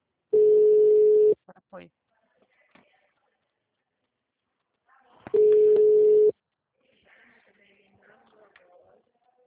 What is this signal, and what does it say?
Ringback tone (Europe, 425 Hz), with speech.
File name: out- -2024- - - .62709